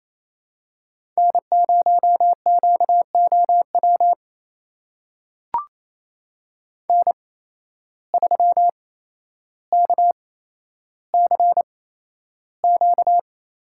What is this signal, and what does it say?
Morse code, heard as a keyed tone, about 28 wpm, spaced as for 12 wpm, 700 Hz. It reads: N0QOW N3KCQ
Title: USA Call Signs - 5 Characters - 8x Character Spacing - 28wpm